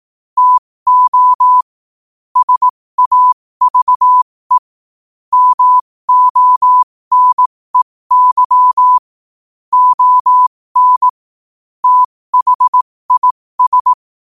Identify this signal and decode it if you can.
Morse code, heard as a keyed tone, about 18 wpm, spaced as for 15 wpm, 1000 Hz. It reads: TO SAVE MONEY ON THIS
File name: qotdfiles / 15wpm